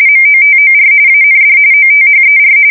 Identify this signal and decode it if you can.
RTTY, about 45.45 baud. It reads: MOTOR AND OTHER